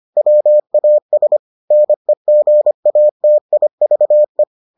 Morse code, heard as a keyed tone, about 25 wpm, 600 Hz. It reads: WAS NEGATIVE